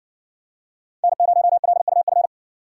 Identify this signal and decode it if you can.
Morse code, heard as a keyed tone, about 56 wpm, 700 Hz. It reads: N0LUF